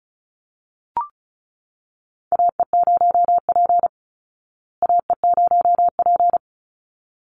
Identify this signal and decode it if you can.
Morse code, heard as a keyed tone, about 34 wpm, 700 Hz. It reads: AE0P AE0P
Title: Call Signs Repeated 3x - 35wpm